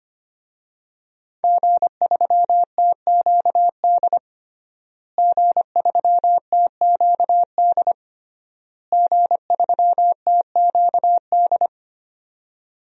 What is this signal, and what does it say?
Morse code, heard as a keyed tone, about 25 wpm, 700 Hz. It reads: G3TQD G3TQD G3TQD
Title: DXCC-100 Call Signs - 5 Characters - Repeated 3x - 25wpm